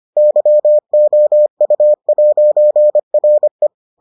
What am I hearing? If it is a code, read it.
Morse code, heard as a keyed tone, about 25 wpm, 600 Hz. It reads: YOU'RE